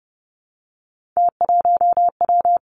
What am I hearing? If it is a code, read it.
Morse code, heard as a keyed tone, about 30 wpm, 700 Hz. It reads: T1W
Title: Three Challenging Letter-Number Combinations 30wpm